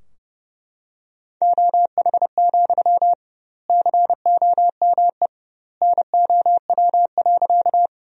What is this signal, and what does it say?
Morse code, heard as a keyed tone, about 30 wpm, 700 Hz. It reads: OH, COME NOW.
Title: 2x Word-Spacing - Sentences from Top 500 Words - 30wpm